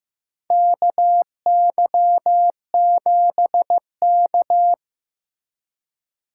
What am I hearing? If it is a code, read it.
Morse code, heard as a keyed tone, about 15 wpm, 700 Hz. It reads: KY7K
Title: K1USN SST QSO Practice - 15wpm